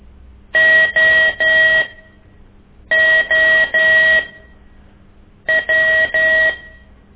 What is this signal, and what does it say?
Morse code, heard as a keyed tone, about 13 wpm, spaced as for 6 wpm, 1900 Hz. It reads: OO W
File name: ik1weg2 / CORSI CW